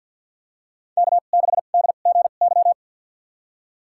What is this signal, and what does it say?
Morse code, heard as a keyed tone, about 50 wpm, spaced as for 36 wpm, 700 Hz. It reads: K6DGY
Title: USA Call Signs - 5 Characters - 2x Character Spacing - 50wpm